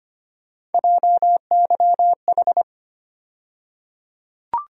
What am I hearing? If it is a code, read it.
Morse code, heard as a keyed tone, about 25 wpm, 700 Hz. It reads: JYH
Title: USA Call Sign Postfixes - 3 Letters 25wpm